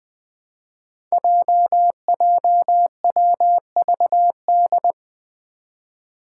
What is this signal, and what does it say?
Morse code, heard as a keyed tone, about 20 wpm, 700 Hz. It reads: JJWVD
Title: Code Groups - 5 Letters - 20wpm